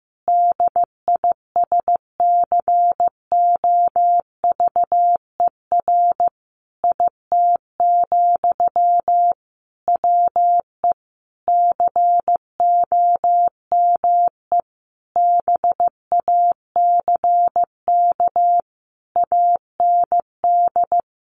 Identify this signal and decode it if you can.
Morse code, heard as a keyed tone, about 14 wpm, 700 Hz. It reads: DISCOVER IT, WE COME BACK AND
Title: Sentences from Top 1100 Words 15wpm